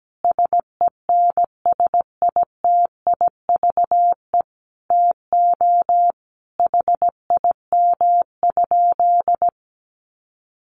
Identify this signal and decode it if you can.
Morse code, heard as a keyed tone, about 17 wpm, 700 Hz. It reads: SENSITIVE TO HIM?
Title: Sentences from Top 1400 Words 17wpm